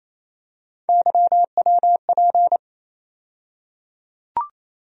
Morse code, heard as a keyed tone, about 28 wpm, 700 Hz. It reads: YWP
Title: USA Call Sign Postfixes - 3 Letters 28wpm